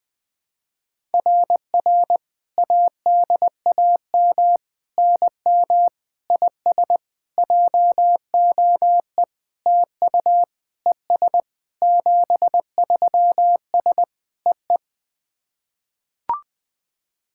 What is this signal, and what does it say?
Morse code, heard as a keyed tone, about 21 wpm, 700 Hz. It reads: RR ADAM NM IS JOE TU ES 73S EE
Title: SOTA QSO Practice 20wpm